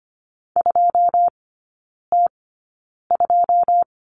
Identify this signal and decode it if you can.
Morse code, heard as a keyed tone, about 25 wpm, spaced as for 9 wpm, 690 Hz. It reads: 2T2
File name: Lesson 14e - Number 2 - Three Characters 25wpm